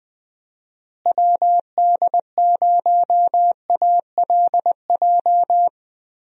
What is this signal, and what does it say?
Morse code, heard as a keyed tone, about 21 wpm, 700 Hz. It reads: WD0ALJ
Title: USA Call Signs - 6 Characters - Repeated 3x - 20wpm